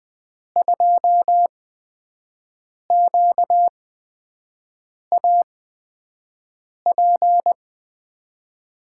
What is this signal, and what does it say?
Morse code, heard as a keyed tone, about 20 wpm, spaced as for 5 wpm, 700 Hz. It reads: 2QAP
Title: USA Call Signs - 6 Characters - 8x Character Spacing - 20wpm